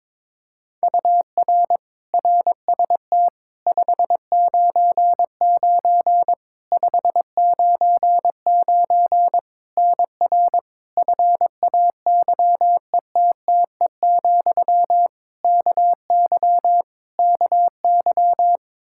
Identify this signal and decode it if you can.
Morse code, heard as a keyed tone, about 22 wpm, 700 Hz. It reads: UR RST 599 599 NR FAYETTE, KY KY
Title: Practice QSO - Exchange 1 - Beginning - 22wpm